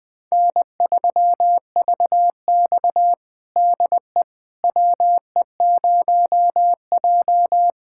Morse code, heard as a keyed tone, about 20 wpm, 700 Hz. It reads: N3VX DE WE0J
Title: USA Callsign DE Callsign - 4 Characters - 20wpm